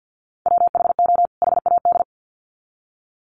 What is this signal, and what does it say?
Morse code, heard as a keyed tone, about 50 wpm, 700 Hz. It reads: WHO HAD
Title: Sets of 2 Words 50wpm